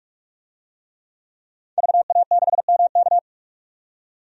Morse code, heard as a keyed tone, about 48 wpm, 700 Hz. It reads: VA6MK
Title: DXCC-100 Call Signs - 5 Characters - 45wpm